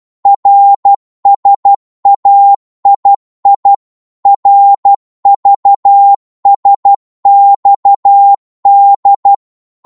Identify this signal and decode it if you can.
Morse code, heard as a keyed tone, about 12 wpm, 800 Hz. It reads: RSAII RVSXD
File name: Morse / ltr-12wpm